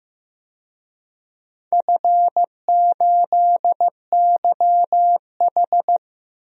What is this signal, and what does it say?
Morse code, heard as a keyed tone, about 15 wpm, 700 Hz. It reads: F8YH
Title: DXCC-100 Call Signs - 4 Characters - Repeated 2x - 15wpm